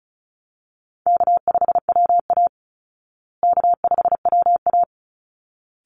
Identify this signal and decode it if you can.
Morse code, heard as a keyed tone, about 34 wpm, 700 Hz. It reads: K5WA K5WA
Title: Easy Call Signs Repeated 2x - 35wpm